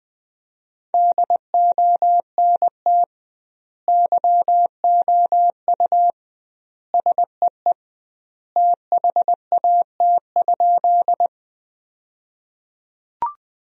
Morse code, heard as a keyed tone, about 20 wpm, 700 Hz. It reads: DONT YOU SEE THAT?
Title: 2x Word-Spacing - Sentences from Top 300 Words - 20wpm